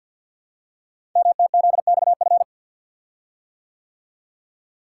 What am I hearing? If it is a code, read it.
Morse code, heard as a keyed tone, about 48 wpm, 700 Hz. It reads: MTZXF
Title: Code Groups - 5 Letters - 50wpm